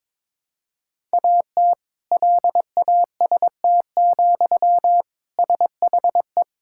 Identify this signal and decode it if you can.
Morse code, heard as a keyed tone, about 22 wpm, 700 Hz. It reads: AT LAST, SHE
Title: Sentences from Top 1600 Words 22wpm